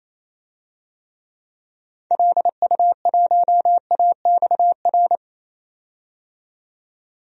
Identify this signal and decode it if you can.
Morse code, heard as a keyed tone, about 28 wpm, 700 Hz. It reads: LU1AXR
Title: DXCC-100 Call Signs - 6 Characters - 28wpm